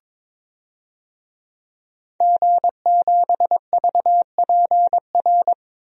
Morse code, heard as a keyed tone, about 22 wpm, 700 Hz. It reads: G7VPR
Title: DXCC-100 Call Signs - 5 Characters - 22wpm